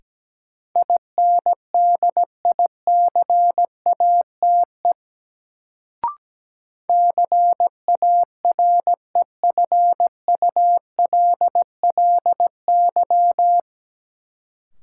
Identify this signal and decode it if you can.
Morse code, heard as a keyed tone, about 17 wpm, 700 Hz. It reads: INDICATE CAREFULLY
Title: Sentences from Top 1300 Words - New Word Drill Alt - 17wpm